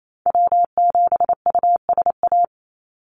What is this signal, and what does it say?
Morse code, heard as a keyed tone, about 28 wpm, 700 Hz. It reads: W7USA
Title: NA SOTA Activator Callsigns 28wpm